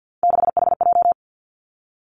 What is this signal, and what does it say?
Morse code, heard as a keyed tone, about 50 wpm, 700 Hz. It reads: 6HJ
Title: Three Challenging Letter-Number Combinations 50wpm